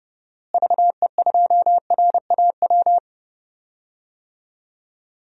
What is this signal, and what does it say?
Morse code, heard as a keyed tone, about 32 wpm, 700 Hz. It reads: VE2RAW
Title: DXCC-100 Call Signs - 6 Characters - Repeated 3x - 30wpm